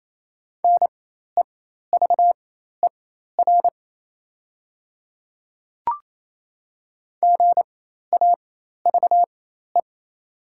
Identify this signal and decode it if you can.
Morse code, heard as a keyed tone, about 28 wpm, spaced as for 13 wpm, 700 Hz. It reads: NEVER GAVE
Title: 4x Character Spacing - Top 500 Words - 28wpm